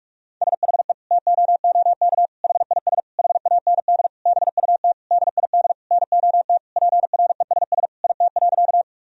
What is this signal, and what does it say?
Morse code, heard as a keyed tone, about 41 wpm, 700 Hz. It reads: SHE TOOK HIS HAND BUT DID NOT PRESS IT.
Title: Sentences from Top 1000 Words - Review Part 1 - 45wpm